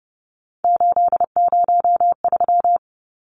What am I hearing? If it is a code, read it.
Morse code, heard as a keyed tone, about 30 wpm, 700 Hz. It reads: 803
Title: Numbers - 3-Digit 30wpm